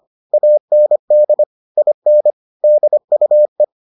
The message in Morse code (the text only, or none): AND IN DUE